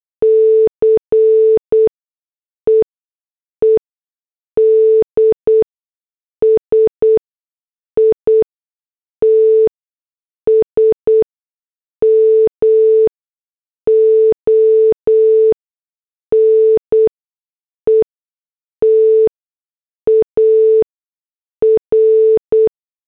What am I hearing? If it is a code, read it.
Morse code, heard as a keyed tone, about 8 wpm, spaced as for 6 wpm, 440 Hz. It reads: CEEDS ITS MONETAR
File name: morse / headlines